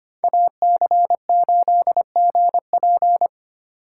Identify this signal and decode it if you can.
Morse code, heard as a keyed tone, about 29 wpm, 700 Hz. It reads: AC8GP
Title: Speed Racing - USA 2x2 Call Signs 25wpm